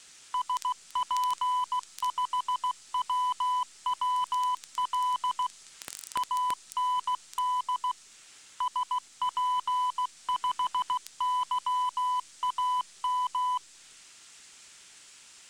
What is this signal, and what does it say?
Morse code, heard as a keyed tone, about 16 wpm, 1000 Hz. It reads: SP5WWL AND SP5YAM